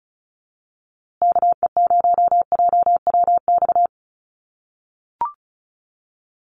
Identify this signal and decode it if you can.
Morse code, heard as a keyed tone, about 34 wpm, 700 Hz. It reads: KE0JWX
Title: USA Call Signs Repeated 2x 35wpm